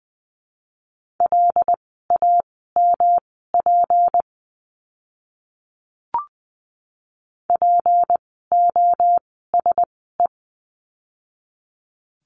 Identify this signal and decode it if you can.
Morse code, heard as a keyed tone, about 20 wpm, spaced as for 14 wpm, 700 Hz. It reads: LAMP POSE
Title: 153 Lesson 25f - Letter p - Words 20wpm